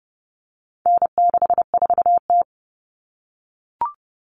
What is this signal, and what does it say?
Morse code, heard as a keyed tone, about 30 wpm, 700 Hz. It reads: N64T
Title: Common Cut Numbers - 4-Digit 30wpm